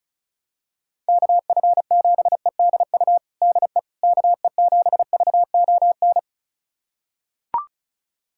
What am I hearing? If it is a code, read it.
Morse code, heard as a keyed tone, about 34 wpm, 700 Hz. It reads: KF7EDU DE KE7VON
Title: USA Callsign DE Callsign - 6 Characters - 35wpm